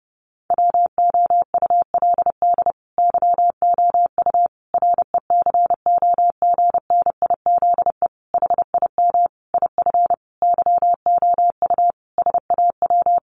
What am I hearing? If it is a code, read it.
Morse code, heard as a keyed tone, about 30 wpm, 700 Hz. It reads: WOULD YOU RECOGNIZE HIM IF YOU SAW
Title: Sentences from Top 1400 Words 30wpm